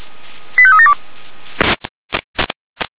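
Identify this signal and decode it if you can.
Five-tone selective call (ZVEI), heard as quick runs of five tones, tones of about 70 ms. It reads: 85372
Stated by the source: ZVEI five-tone alert received by SDR, from audio server of Feuerwehr Orlamünde fire brigade.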